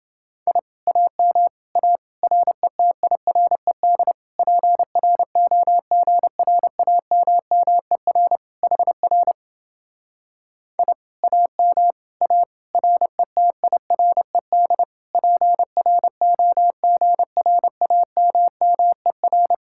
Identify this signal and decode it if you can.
Morse code, heard as a keyed tone, about 29 wpm, 700 Hz. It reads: I AM A RETIRED PROGRAMMER HR I AM A RETIRED PROGRAMMER